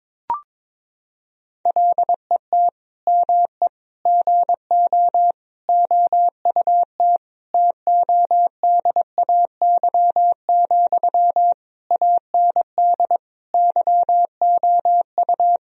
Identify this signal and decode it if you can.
Morse code, heard as a keyed tone, about 22 wpm, 700 Hz. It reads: LET ME GO OUT TODAY, AND YOU
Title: Sentences from Top 1000 Words - Review Part 4 - 22wpm